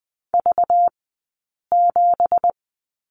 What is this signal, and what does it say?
Morse code, heard as a keyed tone, about 20 wpm, spaced as for 8 wpm, 700 Hz. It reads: V7
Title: 211 Lesson 34d - Number 8 - Two Characters 20wpm